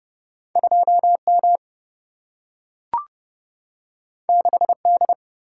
Morse code, heard as a keyed tone, about 30 wpm, 700 Hz. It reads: 2 M 6 D
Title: Single-Letter Single-Number 30wpm